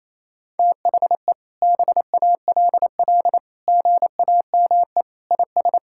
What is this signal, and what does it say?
Morse code, heard as a keyed tone, about 28 wpm, 700 Hz. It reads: THE BALL GAME IS
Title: Sentences from Top 1000 Words - Review Part 6 - 28wpm